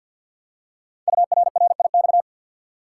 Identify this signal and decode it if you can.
Morse code, heard as a keyed tone, about 48 wpm, 700 Hz. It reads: URRIX
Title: Code Groups - 5 Letters - 50wpm